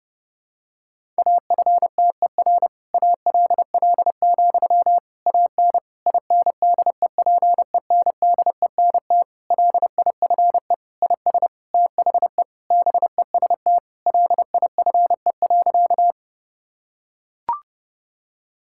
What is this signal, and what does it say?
Morse code, heard as a keyed tone, about 30 wpm, 700 Hz. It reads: AFTER ALL, AN INDEPENDENT LIFE IS THE BEST LIFE.
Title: Sentences from Top 1000 Words - Review Part 6 - 30wpm